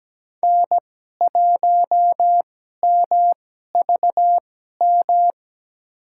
Morse code, heard as a keyed tone, about 17 wpm, spaced as for 12 wpm, 700 Hz. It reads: N1MVM
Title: USA Call Signs - 5 Characters - 2x Character Spacing - 17wpm